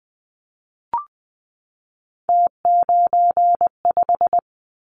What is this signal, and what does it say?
Morse code, heard as a keyed tone, about 20 wpm, 700 Hz. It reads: T95
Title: Callsign Prefixes for Top 50 DXCC - 20wpm